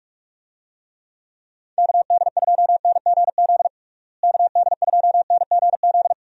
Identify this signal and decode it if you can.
Morse code, heard as a keyed tone, about 42 wpm, 700 Hz. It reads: KD2NGZ KD2NGZ
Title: USA Call Signs - 6 Characters - Repeated 2x - 45wpm